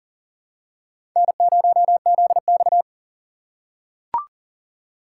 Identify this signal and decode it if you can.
Morse code, heard as a keyed tone, about 40 wpm, 700 Hz. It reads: N0ZX
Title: USA Call Signs - 4 Characters - 4x Character Spacing - 40wpm